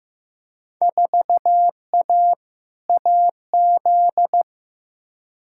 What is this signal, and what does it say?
Morse code, heard as a keyed tone, about 15 wpm, 700 Hz. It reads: 4A AZ
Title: Field Day Exchanges - 15wpm